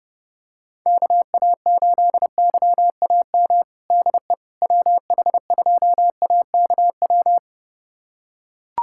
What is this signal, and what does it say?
Morse code, heard as a keyed tone, about 28 wpm, 700 Hz. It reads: KA8YAM DE WH2AKW
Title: USA Callsign DE Callsign - 6 Characters - 30wpm